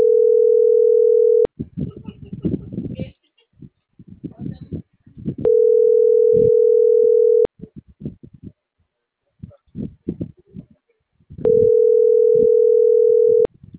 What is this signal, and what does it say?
Ringback tone (North America), 440 + 480 Hz.